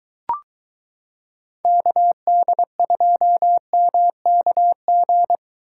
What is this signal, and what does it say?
Morse code, heard as a keyed tone, about 23 wpm, 700 Hz. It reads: KD2MKG